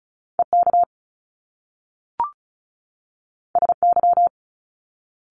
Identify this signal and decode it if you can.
Morse code, heard as a keyed tone, about 35 wpm, 700 Hz. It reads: E K S Y
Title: Top Two Letter Combinations in Words 35wpm